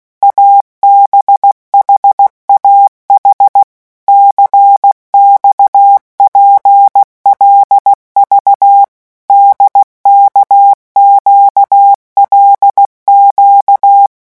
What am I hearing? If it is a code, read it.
Morse code, heard as a keyed tone, about 16 wpm, 800 Hz. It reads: ABHAH CXPLV DKQLQ
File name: morse mp3 / revision5-vitesse 15 mots